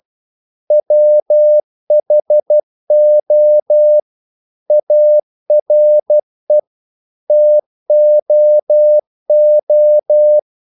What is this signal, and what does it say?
Morse code, heard as a keyed tone, about 12 wpm, 600 Hz. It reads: WHO ARE TOO
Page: English quotations in CW (telegraphy - morse)